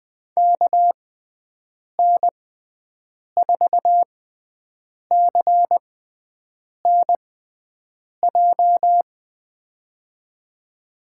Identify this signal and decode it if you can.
Morse code, heard as a keyed tone, about 21 wpm, spaced as for 7 wpm, 700 Hz. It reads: KN4CNJ
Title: USA Call Signs - 6 Characters - 6x Character Spacing - 20wpm